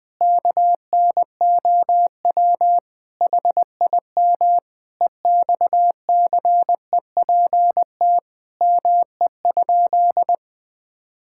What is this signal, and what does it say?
Morse code, heard as a keyed tone, about 20 wpm, 700 Hz. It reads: KNOW HIM EXCEPT ME?
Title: Sentences from Top 1600 Words 20wpm